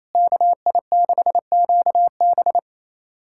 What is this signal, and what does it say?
Morse code, heard as a keyed tone, about 27 wpm, 700 Hz. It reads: KI6QB
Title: USA Callsign DE Callsign - 5 Characters - 28wpm